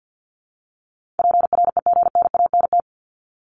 Morse code, heard as a keyed tone, about 50 wpm, 700 Hz. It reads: PREGNANT
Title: Top 751-1000 Words 50wpm